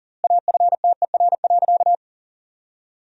Morse code, heard as a keyed tone, about 40 wpm, 700 Hz. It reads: AFTER.
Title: 3x Word-Spacing - Sentences from Top 500 Words - 40wpm